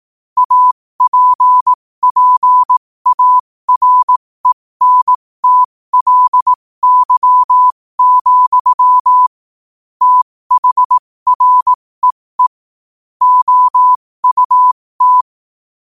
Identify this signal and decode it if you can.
Morse code, heard as a keyed tone, about 18 wpm, spaced as for 15 wpm, 1000 Hz. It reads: APPARENTLY, THREE OUT
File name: qotdfiles / 15wpm